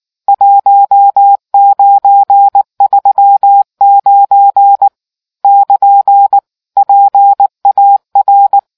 Morse code, heard as a keyed tone, about 19 wpm, 800 Hz. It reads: 1939 <KN> PAR